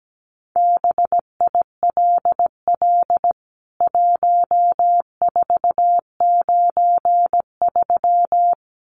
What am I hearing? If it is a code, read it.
Morse code, heard as a keyed tone, about 17 wpm, 700 Hz. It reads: BILL 1493
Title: CWT Exchange 17wpm